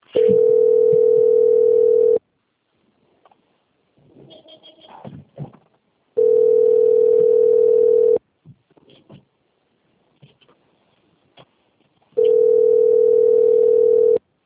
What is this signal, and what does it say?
Ringback tone (North America), 440 + 480 Hz.
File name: out- - - - - .10242